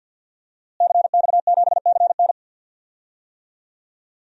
Morse code, heard as a keyed tone, about 48 wpm, 700 Hz. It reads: CX7CN
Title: DXCC-100 Call Signs - 5 Characters - Repeated 2x - 50wpm